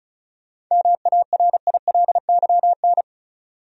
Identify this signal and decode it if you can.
Morse code, heard as a keyed tone, about 36 wpm, 700 Hz. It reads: MARILYN
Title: USA Common Names - 35wpm